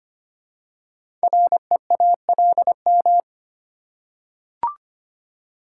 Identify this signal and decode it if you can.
Morse code, heard as a keyed tone, about 25 wpm, 700 Hz. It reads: REALM
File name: 5 Letter Words - First 3 Letters Given - 25wpm